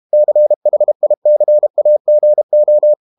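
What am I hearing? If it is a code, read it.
Morse code, heard as a keyed tone, about 32 wpm, 600 Hz. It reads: CHICAGO
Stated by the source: Morse code practice at 32 wpm from CW med Gnister, reading Ken Follett's novel Never in English.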